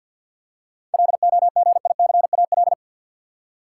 Morse code, heard as a keyed tone, about 53 wpm, 700 Hz. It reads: LOGICAL
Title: Seven Letter Words - 50wpm